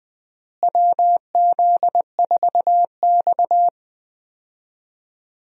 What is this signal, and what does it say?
Morse code, heard as a keyed tone, about 20 wpm, 700 Hz. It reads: WZ4X
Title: USA Call Signs - 4 Characters - Repeated 3x - 20wpm